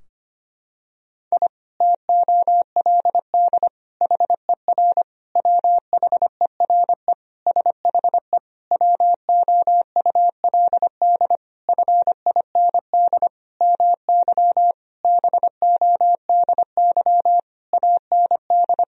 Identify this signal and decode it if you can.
Morse code, heard as a keyed tone, about 25 wpm, 700 Hz. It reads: I TOLD HER WHERE SHE WOULD FIND MY BODY AND
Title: Sentences from Top 1000 Words - Review Part 2 - 25wpm